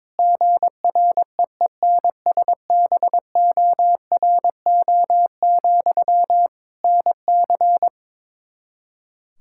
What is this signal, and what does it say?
Morse code, heard as a keyed tone, about 21 wpm, 700 Hz. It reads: GREENSBORO, NC
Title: Practice QSO - Exchange 1 - Beginning - 22wpm